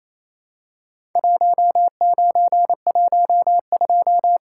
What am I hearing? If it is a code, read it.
Morse code, heard as a keyed tone, about 28 wpm, 700 Hz. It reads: 1912
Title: Worked All Europe DX Contest - QTC Exchange - 28wpm